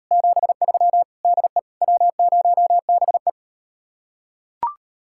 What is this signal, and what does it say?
Morse code, heard as a keyed tone, about 38 wpm, 700 Hz. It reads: 73 DE W0BE